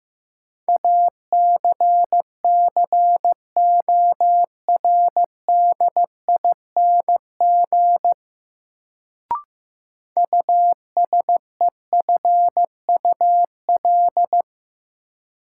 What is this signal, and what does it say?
Morse code, heard as a keyed tone, about 15 wpm, 700 Hz. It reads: ACCORDING USEFUL
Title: Sentences from Top 700 Words - New Word Drill - 15wpm